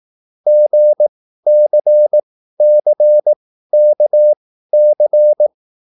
Morse code, heard as a keyed tone, about 18 wpm, spaced as for 13 wpm, 600 Hz. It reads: GCCKC
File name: QSL / CW lesson